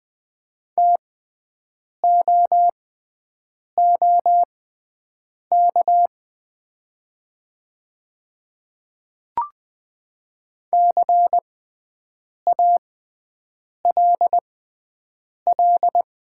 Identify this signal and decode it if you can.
Morse code, heard as a keyed tone, about 20 wpm, spaced as for 7 wpm, 700 Hz. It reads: TOOK CALL
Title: 6x Character Spacing - Top 300 Words - 20wpm